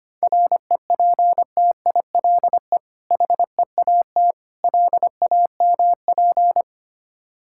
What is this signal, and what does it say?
Morse code, heard as a keyed tone, about 25 wpm, 700 Hz. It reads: REPTILE HEAT LAMP
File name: practice / 25wpm / Difficult Words with Hints - 25wpm